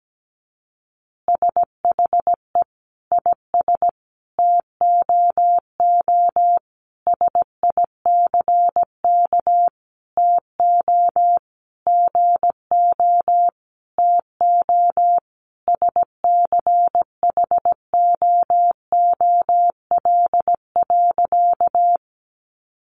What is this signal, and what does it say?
Morse code, heard as a keyed tone, about 17 wpm, 700 Hz. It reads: SHE IS TOO SICK TO GO TO SCHOOL.
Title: Easy Phrases 17wpm